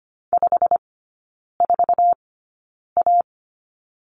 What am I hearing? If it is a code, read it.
Morse code, heard as a keyed tone, about 25 wpm, spaced as for 9 wpm, 700 Hz. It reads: 54A
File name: Lesson 17e - Letter u - Three Characters 25wpm